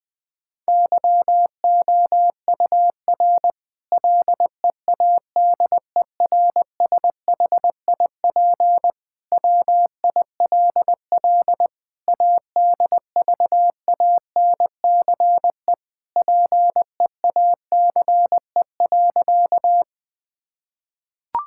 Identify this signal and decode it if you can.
Morse code, heard as a keyed tone, about 21 wpm, 700 Hz. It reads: YOUR LEADERSHIP WILL ADVANCE PEACE.
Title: Sentences from Top 1600 Words 20wpm